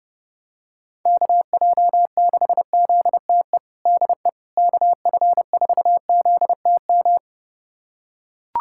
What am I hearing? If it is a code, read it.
Morse code, heard as a keyed tone, about 30 wpm, 700 Hz. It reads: KJ6ZTE DE KF4ZTM
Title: USA Callsign DE Callsign - 6 Characters - 30wpm